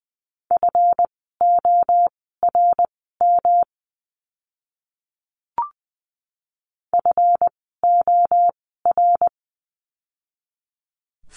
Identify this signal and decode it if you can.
Morse code, heard as a keyed tone, about 20 wpm, spaced as for 14 wpm, 700 Hz. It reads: FORM FOR
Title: 139 Lesson 23f - Letter f - Words 20wpm